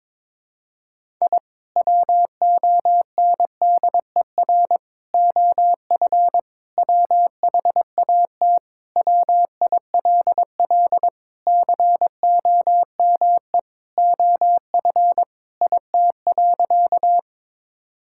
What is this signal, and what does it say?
Morse code, heard as a keyed tone, about 22 wpm, 700 Hz. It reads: I WONDER OF WHAT WILL COME OF IT.
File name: practice / 22wpm / Sentences with 2-Word Phrases - 22wpm